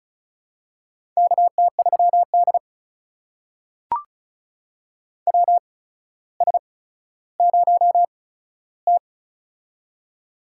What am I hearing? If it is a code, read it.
Morse code, heard as a keyed tone, about 36 wpm, spaced as for 15 wpm, 700 Hz. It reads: KT3D WS0T
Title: USA Call Signs - 4 Characters - 8x Character Spacing - 35wpm